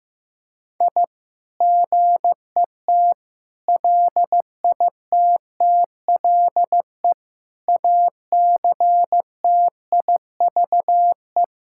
Morse code, heard as a keyed tone, about 15 wpm, 700 Hz. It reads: I GET LITTLE ACTIVE
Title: Sentences from Top 1000 Words - Review Part 6 - 15wpm